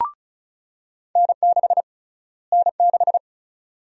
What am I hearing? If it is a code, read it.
Morse code, heard as a keyed tone, about 36 wpm, 700 Hz. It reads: N6 N6
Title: USA Call Sign Prefixes - Repeated 2x - 35wpm